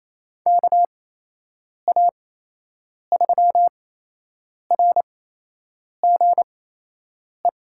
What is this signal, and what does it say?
Morse code, heard as a keyed tone, about 28 wpm, spaced as for 8 wpm, 700 Hz. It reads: KA3RGE
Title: USA Call Signs - 8x Character Spacing - 28wpm